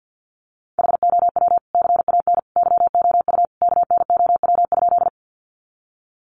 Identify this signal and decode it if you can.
Morse code, heard as a keyed tone, about 50 wpm, 700 Hz. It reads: HOW CAN YOU KNOW?
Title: Sentences from Top 100 Words 50wpm